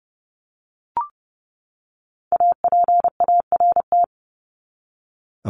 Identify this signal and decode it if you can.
Morse code, heard as a keyed tone, about 30 wpm, 700 Hz. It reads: APART
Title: Top 1501-1750 Words 30wpm